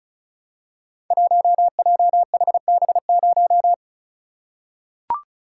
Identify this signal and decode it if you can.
Morse code, heard as a keyed tone, about 34 wpm, 700 Hz. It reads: 1JHB0
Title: Code Groups - 5 Letter:Numbers - 35wpm